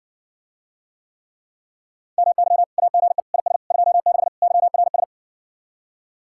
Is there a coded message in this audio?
Yes: Morse code from 2.2 to 5.0 s, heard as a keyed tone, about 58 wpm, 700 Hz. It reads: MY AGE IS 26 YRS